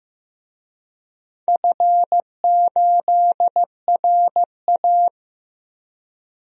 Morse code, heard as a keyed tone, about 16 wpm, 700 Hz. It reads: F8RA
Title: DXCC-100 Call Signs - 4 Characters - Repeated 2x - 15wpm